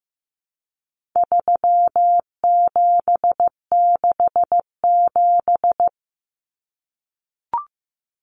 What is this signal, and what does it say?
Morse code, heard as a keyed tone, about 15 wpm, 700 Hz. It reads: 3767
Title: Numbers - 4-Digit 15wpm